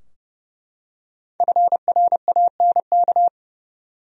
Morse code, heard as a keyed tone, about 30 wpm, 700 Hz. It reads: FRANK
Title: USA Common Names - Max 5 Letters - 30wpm